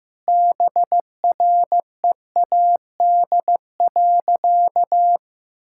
Morse code, heard as a keyed tone, about 15 wpm, 700 Hz. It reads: BREAD.
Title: Sentences from Top 1600 Words 15wpm